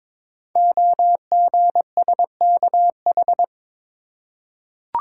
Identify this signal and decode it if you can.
Morse code, heard as a keyed tone, about 22 wpm, 700 Hz. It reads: OGSKH
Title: Code Groups - 5 Letters - 22wpm